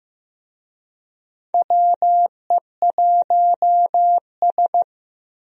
Morse code, heard as a keyed tone, about 15 wpm, 700 Hz. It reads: WE1S
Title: USA Call Signs - 4 Characters - Repeated 2x - 15wpm